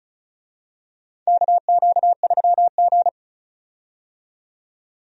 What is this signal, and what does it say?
Morse code, heard as a keyed tone, about 36 wpm, 700 Hz. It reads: KQ3G
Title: USA Call Signs - 4 Characters - 35wpm